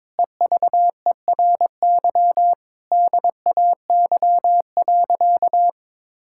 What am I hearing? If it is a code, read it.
Morse code, heard as a keyed tone, about 22 wpm, 700 Hz. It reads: EVERY DAY.
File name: Difficult Sentences with Hints - 22wpm